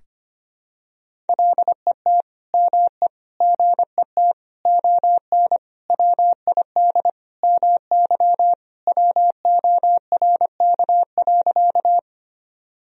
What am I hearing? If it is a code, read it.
Morse code, heard as a keyed tone, about 25 wpm, 700 Hz. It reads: LET ME GET ON WID MY WORK.
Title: Sentences from Top 300 Words and Abbreviations 25wpm